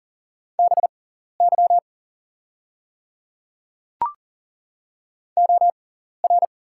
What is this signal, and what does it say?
Morse code, heard as a keyed tone, about 40 wpm, spaced as for 14 wpm, 690 Hz. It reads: BY OR
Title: 6x Character Spacing - Top 100 Words - 40wpm